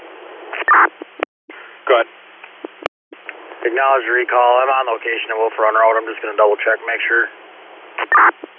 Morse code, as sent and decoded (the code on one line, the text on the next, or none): none